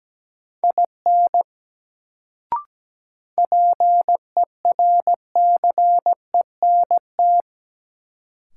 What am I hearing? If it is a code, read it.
Morse code, heard as a keyed tone, about 17 wpm, 700 Hz. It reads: IN PERCENT
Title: Sets of 2 Words - New Word Drill - 17wpm